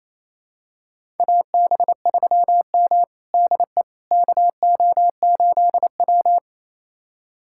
Morse code, heard as a keyed tone, about 28 wpm, 700 Hz. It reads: AB3M DE KO8W
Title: USA Callsign DE Callsign - 4 Characters - 28wpm